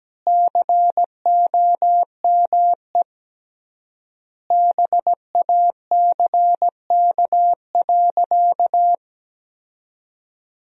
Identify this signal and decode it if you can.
Morse code, heard as a keyed tone, about 17 wpm, 700 Hz. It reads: COME BACK.
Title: 3x Word-Spacing - Sentences from Top 300 Words - 17wpm